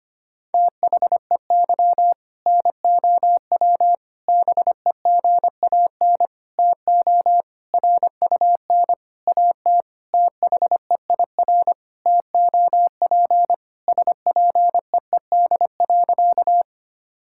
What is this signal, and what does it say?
Morse code, heard as a keyed tone, about 25 wpm, 700 Hz. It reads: THEY NOW BEGAN TO RUN AT THEIR TOP SPEED.
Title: Sentences from Top 1000 Words - Review Part 5 - 25wpm